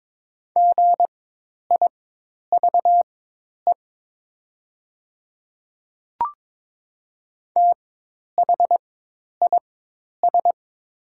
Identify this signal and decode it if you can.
Morse code, heard as a keyed tone, about 22 wpm, spaced as for 10 wpm, 700 Hz. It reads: GIVE THIS
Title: 4x Character Spacing - Top 200 Words - 22wpm